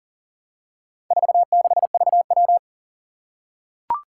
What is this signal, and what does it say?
Morse code, heard as a keyed tone, about 38 wpm, 700 Hz. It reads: 46VW
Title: Four Challenging Letter-Number Combinations 40wpm